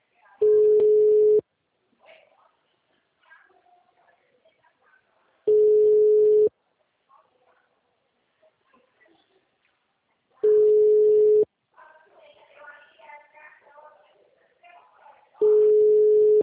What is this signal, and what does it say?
Ringback tone (Europe, 425 Hz), with speech.